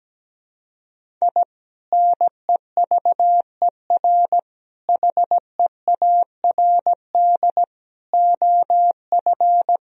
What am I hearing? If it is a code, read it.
Morse code, heard as a keyed tone, about 17 wpm, 700 Hz. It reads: I NEVER HEARD OF
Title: Sentences from Top 1000 Words - Review Part 1 - 17wpm